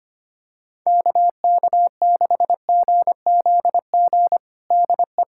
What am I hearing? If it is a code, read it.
Morse code, heard as a keyed tone, about 25 wpm, 700 Hz. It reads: KK6GZG DE
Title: USA Callsign DE Callsign - 6 Characters - 25wpm